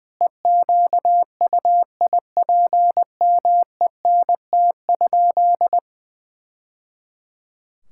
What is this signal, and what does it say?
Morse code, heard as a keyed tone, about 21 wpm, 700 Hz. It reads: EQUIPMENT?
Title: Sentences from Top 1000 Words - Review Part 6 - 20wpm